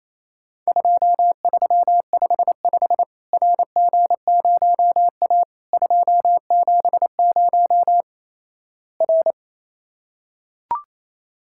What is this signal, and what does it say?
Morse code, heard as a keyed tone, about 28 wpm, 700 Hz. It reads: 2355 RG0A 270 R
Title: Worked All Europe DX Contest - QTC Exchange - 28wpm